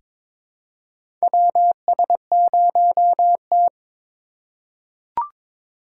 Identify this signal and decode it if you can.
Morse code, heard as a keyed tone, about 22 wpm, 700 Hz. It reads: WS0T
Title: USA Call Signs - 4 Characters - 2x Character Spacing - 22wpm